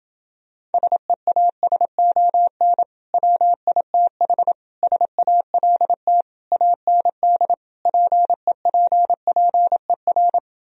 Morse code, heard as a keyed tone, about 30 wpm, 700 Hz. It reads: SEASON WITH SALT AND PEPPER